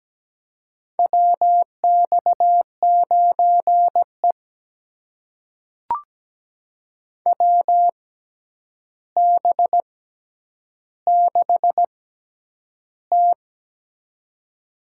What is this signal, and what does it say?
Morse code, heard as a keyed tone, about 17 wpm, spaced as for 9 wpm, 700 Hz. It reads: WX9E WB6T
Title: USA Call Signs - 4 Characters - 6x Character Spacing - 17wpm